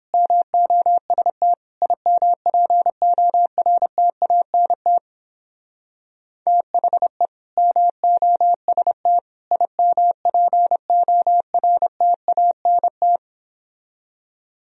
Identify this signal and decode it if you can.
Morse code, heard as a keyed tone, about 29 wpm, 700 Hz. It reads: MOST IMPORTANT THE MOST IMPORTANT